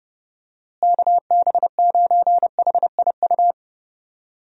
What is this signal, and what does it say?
Morse code, heard as a keyed tone, about 30 wpm, 700 Hz. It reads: KB9HIU
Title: USA Call Signs - 6 Characters - Repeated 3x - 30wpm